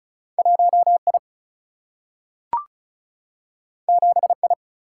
Morse code, heard as a keyed tone, about 36 wpm, 700 Hz. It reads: 1I 7I
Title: Single-Letter Single-Number 35wpm